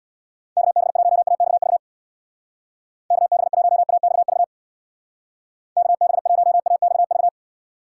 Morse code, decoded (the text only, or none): KB1AXV KB1AXV KB1AXV